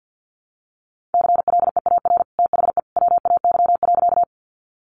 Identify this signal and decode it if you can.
Morse code, heard as a keyed tone, about 50 wpm, 700 Hz. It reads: CLEAR THE WAY.
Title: Sentences from Top 500 Words 50wpm